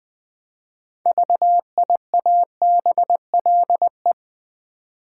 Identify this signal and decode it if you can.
Morse code, heard as a keyed tone, about 20 wpm, 700 Hz. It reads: VIABLE
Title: 6 Letter Words - First 3 Letters Given - 20wpm